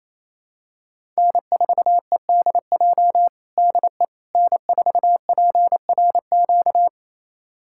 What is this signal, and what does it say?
Morse code, heard as a keyed tone, about 28 wpm, 700 Hz. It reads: N4EDJ DE N4PRQ
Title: USA Callsign DE Callsign - 5 Characters - 28wpm